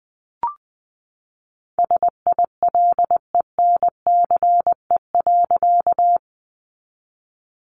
Morse code, heard as a keyed tone, about 20 wpm, 700 Hz. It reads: SILENCE.
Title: A Sound Of Thunder 20wpm